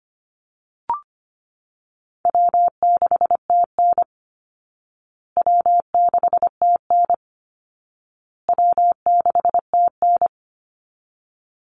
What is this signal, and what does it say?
Morse code, heard as a keyed tone, about 25 wpm, 700 Hz. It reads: W6TN W6TN W6TN
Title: Easy Call Signs Repeated 3x - 25wpm